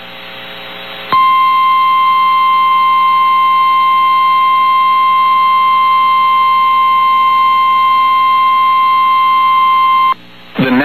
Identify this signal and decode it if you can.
Emergency Alert System (SAME) tones (the 1050 Hz attention tone). It is heard from 1.1 to 10.2 s.
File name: eas received alerts / L1 19549 audio